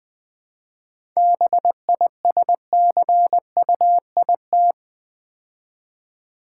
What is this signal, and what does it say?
Morse code, heard as a keyed tone, about 20 wpm, 690 Hz. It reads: BISCUIT
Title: 7 Letter Words - First 3 Letters Given - 20wpm